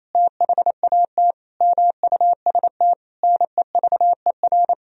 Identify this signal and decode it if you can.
Morse code, heard as a keyed tone, about 28 wpm, 700 Hz. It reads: THAT MUST NEVER
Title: Sentences from Top 1000 Words - Review Part 3 - 28wpm